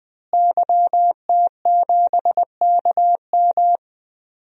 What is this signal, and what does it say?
Morse code, heard as a keyed tone, about 20 wpm, 700 Hz. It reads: YT7KM
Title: DXCC-100 Call Signs - 5 Characters - Repeated 2x - 20wpm